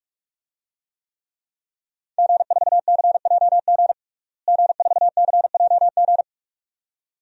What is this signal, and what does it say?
Morse code, heard as a keyed tone, about 41 wpm, 700 Hz. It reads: G4CJG G4CJG
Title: DXCC-100 Call Signs - 5 Characters - Repeated 2x - 45wpm